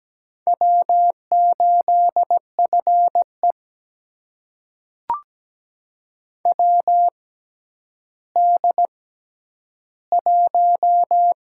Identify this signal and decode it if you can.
Morse code, heard as a keyed tone, about 17 wpm, spaced as for 10 wpm, 700 Hz. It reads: W8FE WD1
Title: USA Call Signs - 4 Characters - 6x Character Spacing - 17wpm